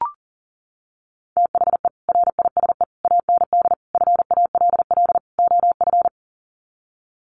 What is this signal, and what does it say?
Morse code, heard as a keyed tone, about 40 wpm, 700 Hz. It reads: THE RISE AND FALL OF
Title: Sets of 5 Words Encore2 40wpm